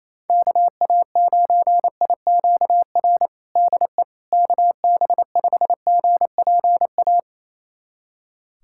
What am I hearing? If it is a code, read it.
Morse code, heard as a keyed tone, about 28 wpm, 700 Hz. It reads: KA9IQR DE KB5GPA
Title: USA Callsign DE Callsign - 6 Characters - 28wpm